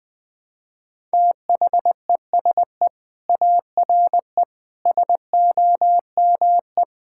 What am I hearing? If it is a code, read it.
Morse code, heard as a keyed tone, about 20 wpm, 700 Hz. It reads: THESE ARE SOME
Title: Sentences with 4-Word Phrases - Spoken Beginning - 20wpm